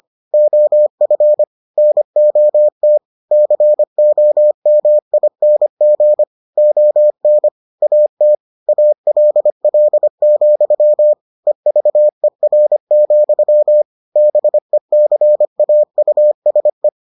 Morse code, heard as a keyed tone, about 25 wpm, 600 Hz. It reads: OF NOT COMING ON AT ALL, EVER, BECAUSE